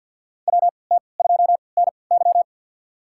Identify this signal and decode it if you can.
Morse code, heard as a keyed tone, about 53 wpm, spaced as for 29 wpm, 700 Hz. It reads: WT2NY
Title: USA Call Signs - 3x Character Spacing - 50wpm